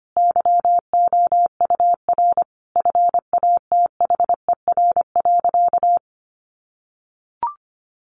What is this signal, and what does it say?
Morse code, heard as a keyed tone, about 25 wpm, 700 Hz. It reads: YOUR FATHER.
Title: Sentences from Top 300 Words 25wpm